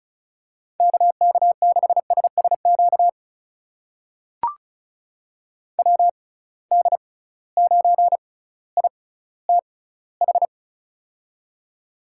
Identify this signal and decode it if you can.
Morse code, heard as a keyed tone, about 35 wpm, spaced as for 18 wpm, 700 Hz. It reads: KK6SSQ WD9ITH
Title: USA Call Signs - 6 Characters - 6x Character Spacing - 35wpm